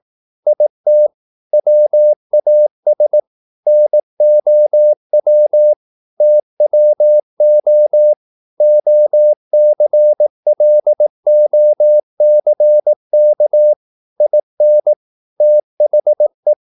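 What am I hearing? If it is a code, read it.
Morse code, heard as a keyed tone, about 18 wpm, 600 Hz. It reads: IT WAS NOW TWO OCLOCK IN THE
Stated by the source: Morse practice at 18 wpm, English text, from CW med Gnister.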